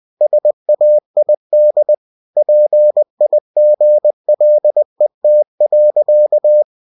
Morse code, heard as a keyed tone, about 20 wpm, 600 Hz. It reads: SAID PIGLET.